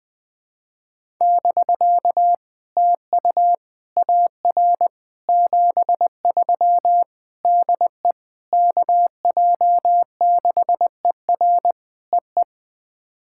Morse code, heard as a keyed tone, about 20 wpm, 700 Hz. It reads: <BK> TU AR 73 DE KJ6ER EE
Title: POTA QSO Practice - 20wpm